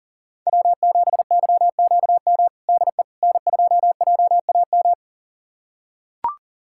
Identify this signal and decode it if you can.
Morse code, heard as a keyed tone, about 40 wpm, 700 Hz. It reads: W7YQM DE N2JAM
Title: USA Callsign DE Callsign - 5 Characters - 40wpm